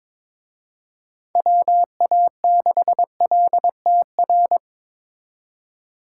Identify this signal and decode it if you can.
Morse code, heard as a keyed tone, about 25 wpm, 700 Hz. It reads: WA6LTR
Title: Speed Racing - USA Call Signs 22wpm